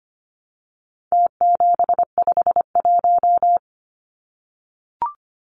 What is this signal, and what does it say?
Morse code, heard as a keyed tone, about 25 wpm, 700 Hz. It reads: T751
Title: Numbers - Contest Style - 4-Digit 25wpm